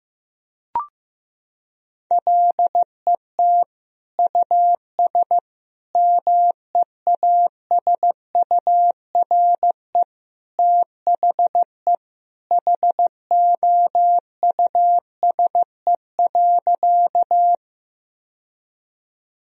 Morse code, heard as a keyed tone, about 16 wpm, 700 Hz. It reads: LET US MEASURE THE HOUSE.
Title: Sentences from Top 1000 Words - Review Part 2 - 15wpm